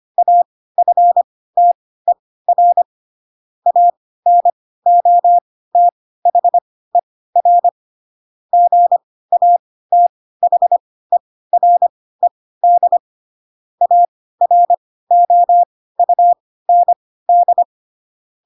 Morse code, heard as a keyed tone, about 25 wpm, spaced as for 16 wpm, 700 Hz. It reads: AFTER ANOTHER GATHERED AROUND
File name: Pinocchio / Pinocchio ch3 10 25 ik1hge